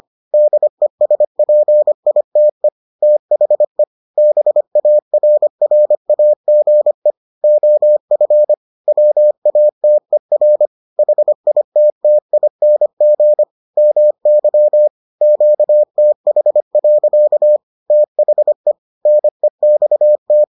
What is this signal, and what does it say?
Morse code, heard as a keyed tone, about 25 wpm, 600 Hz. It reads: DESPITE THE BARRAGE OF WATER HITTING MY QTH. THE NEXT